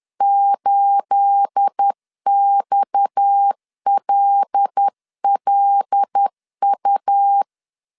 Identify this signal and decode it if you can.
Morse code, heard as a keyed tone, about 10 wpm, 800 Hz. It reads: 8XLLU
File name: CW / DET / 1 9 1500C S A 10wpm 4511